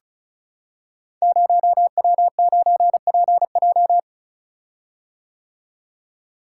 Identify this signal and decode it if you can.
Morse code, heard as a keyed tone, about 36 wpm, 700 Hz. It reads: 0W9PJ
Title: Code Groups - 5 Letter:Numbers - 35wpm